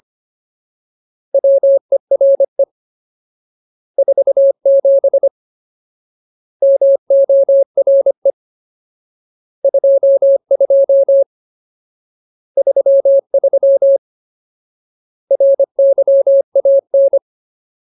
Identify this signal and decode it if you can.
Morse code, heard as a keyed tone, about 25 wpm, 550 Hz. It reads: WERE 47 MORE 22 33 RYAN
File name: Ultimate Word List / 25WPM Ultimate